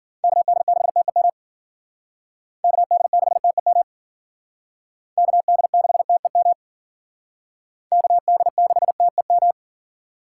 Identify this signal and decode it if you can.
Morse code, heard as a keyed tone, about 50 wpm, 700 Hz. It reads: KD6TEM KD6TEM KD6TEM KD6TEM
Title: Speed Racing - USA 2x3 Call Signs 40wpm